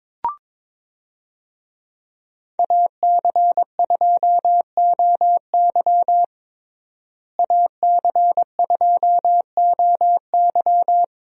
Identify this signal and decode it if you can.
Morse code, heard as a keyed tone, about 22 wpm, 700 Hz. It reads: AC2OY AC2OY
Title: USA Call Signs - 5 Characters - Repeated 2x - 22wpm